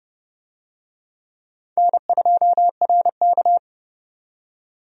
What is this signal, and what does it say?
Morse code, heard as a keyed tone, about 30 wpm, 700 Hz. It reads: N2RK
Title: USA Call Signs - 4 Characters - 30wpm